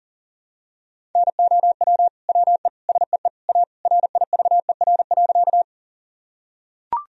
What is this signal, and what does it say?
Morse code, heard as a keyed tone, about 40 wpm, 700 Hz. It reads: NOW WE SEE A RIVER.
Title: Sentences from Top 1000 Words - Review Part 1 - 40wpm